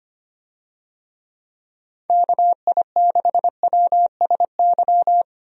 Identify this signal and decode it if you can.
Morse code, heard as a keyed tone, about 25 wpm, 700 Hz. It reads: KI6WSY
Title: USA Call Signs - 6 Characters - 25wpm